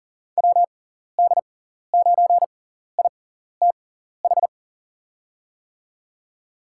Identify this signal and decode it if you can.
Morse code, heard as a keyed tone, about 40 wpm, spaced as for 14 wpm, 700 Hz. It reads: WD9ITH
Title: USA Call Signs - 6 Characters - 6x Character Spacing - 40wpm